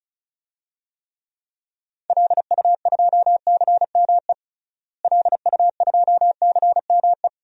Morse code, heard as a keyed tone, about 34 wpm, 700 Hz. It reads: LU2CME LU2CME
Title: DXCC-100 Call Signs - 6 Characters - Repeated 3x - 35wpm